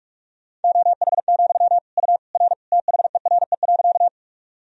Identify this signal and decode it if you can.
Morse code, heard as a keyed tone, about 46 wpm, 700 Hz. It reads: OH, U R THERE.
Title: Sentences from Top 500 Words and Abbreviations 45wpm